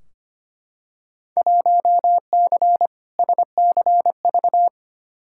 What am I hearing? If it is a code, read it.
Morse code, heard as a keyed tone, about 25 wpm, 700 Hz. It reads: 1C SCV
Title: Field Day Exchanges - 25wpm